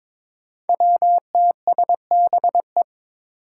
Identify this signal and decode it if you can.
Morse code, heard as a keyed tone, about 22 wpm, 700 Hz. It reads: WTSBE
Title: Five Challenging Letter-Number Combinations 22wpm